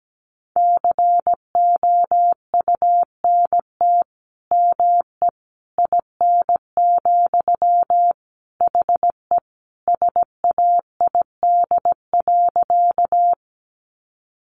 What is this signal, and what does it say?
Morse code, heard as a keyed tone, about 17 wpm, 700 Hz. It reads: COUNT ME IN, HE SAID.
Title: Sentences from Top 1500 Words 17wpm